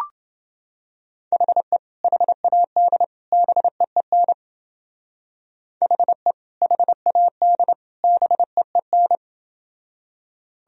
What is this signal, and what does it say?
Morse code, heard as a keyed tone, about 28 wpm, 700 Hz. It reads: HE HAD BEEN HE HAD BEEN